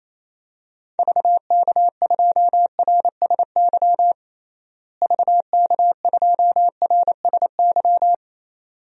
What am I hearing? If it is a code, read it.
Morse code, heard as a keyed tone, about 28 wpm, 700 Hz. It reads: VK2RSY VK2RSY
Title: DXCC-100 Call Signs - 6 Characters - Repeated 2x - 28wpm